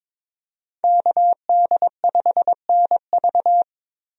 Morse code, heard as a keyed tone, about 22 wpm, 700 Hz. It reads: KD5NV
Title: USA Call Signs - 5 Characters - Repeated 3x - 22wpm